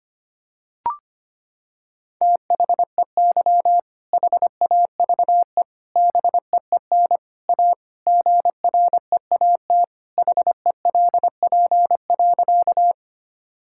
Morse code, heard as a keyed tone, about 25 wpm, 700 Hz. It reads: THEY HAVE BEEN A GREAT HELP.
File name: practice / 25wpm / Sentences with 3-Word Phrases - 25wpm